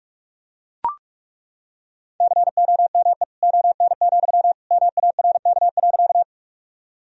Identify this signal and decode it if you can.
Morse code, heard as a keyed tone, about 43 wpm, 700 Hz. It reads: COME ON, MARK.
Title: Sentences from Top 1000 Words - Review Part 1 - 45wpm